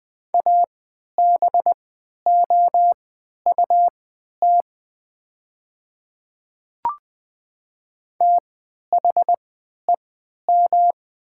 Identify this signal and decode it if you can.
Morse code, heard as a keyed tone, about 20 wpm, spaced as for 11 wpm, 700 Hz. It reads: ABOUT THEM
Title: 3x Character Spacing - Top 100 Words - 20wpm